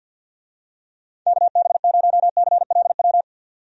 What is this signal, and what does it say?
Morse code, heard as a keyed tone, about 51 wpm, 700 Hz. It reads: KB0CLW
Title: USA Call Signs - 6 Characters - 50wpm